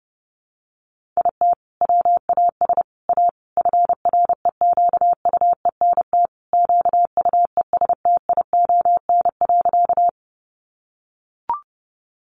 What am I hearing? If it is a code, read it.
Morse code, heard as a keyed tone, about 28 wpm, 700 Hz. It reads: IT WAS A FREQUENT QUESTION.
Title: Sentences from Top 1400 Words 30wpm